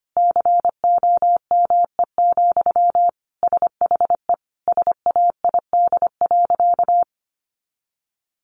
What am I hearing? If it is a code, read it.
Morse code, heard as a keyed tone, about 25 wpm, 700 Hz. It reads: COME, SHE SAID.
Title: Sentences from Top 100 Words 25wpm